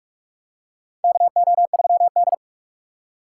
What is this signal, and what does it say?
Morse code, heard as a keyed tone, about 45 wpm, 690 Hz. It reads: KO3D
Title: USA Call Signs - 4 Characters - 4x Character Spacing - 45wpm